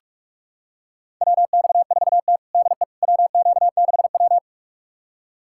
Morse code, heard as a keyed tone, about 42 wpm, 700 Hz. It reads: WX4T DE WQ6W
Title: USA Callsign DE Callsign - 4 Characters - 45wpm